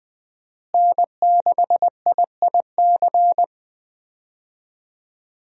Morse code, heard as a keyed tone, about 20 wpm, 700 Hz. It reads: N6IIC
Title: USA Call Signs - 5 Characters - Repeated 3x - 20wpm